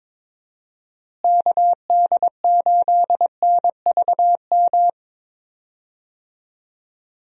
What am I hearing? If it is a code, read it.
Morse code, heard as a keyed tone, about 22 wpm, 700 Hz. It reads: KD8NVM
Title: USA Call Signs - 6 Characters - Repeated 2x - 22wpm